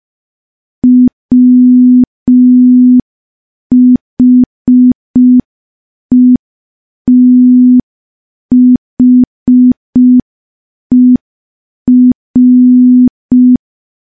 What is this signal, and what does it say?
Morse code, heard as a keyed tone, about 5 wpm, 250 Hz. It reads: WHETHER
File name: morse / morse-training-05WPM